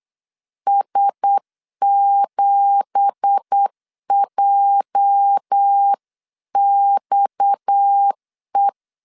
Morse code, heard as a keyed tone, about 8 wpm, 800 Hz. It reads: S7JXE
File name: 8wpm